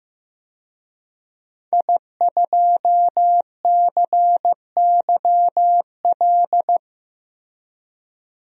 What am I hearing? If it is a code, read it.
Morse code, heard as a keyed tone, about 15 wpm, 700 Hz. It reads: I2CYL
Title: DXCC-100 Call Signs - 5 Characters - Repeated 3x - 15wpm